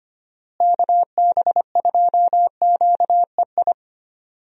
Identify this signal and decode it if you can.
Morse code, heard as a keyed tone, about 25 wpm, 700 Hz. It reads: KB2QEI
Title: USA Call Signs - 6 Characters - Repeated 2x - 25wpm